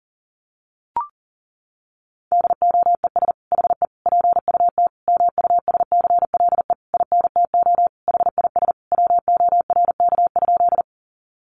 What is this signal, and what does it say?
Morse code, heard as a keyed tone, about 40 wpm, 700 Hz. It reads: DOES HE PUT MUSCLE INTO HIS WORK?
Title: Sentences from Top 1300 Words 40wpm